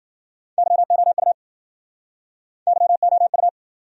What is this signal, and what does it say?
Morse code, heard as a keyed tone, about 58 wpm, 700 Hz. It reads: YOU YOU